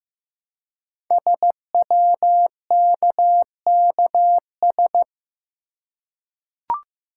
Morse code, heard as a keyed tone, about 15 wpm, 700 Hz. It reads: SWKKS
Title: Code Groups - 5 Letters - 15wpm